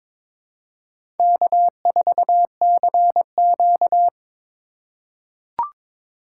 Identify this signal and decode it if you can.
Morse code, heard as a keyed tone, about 22 wpm, 700 Hz. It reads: K4CQ
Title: USA Call Signs - 4 Characters - 4x Character Spacing - 22wpm